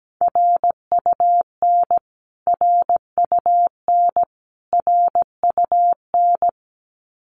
Morse code, heard as a keyed tone, about 17 wpm, 700 Hz. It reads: RUN RUN RUN
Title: Top 500 Words Repeated 17wpm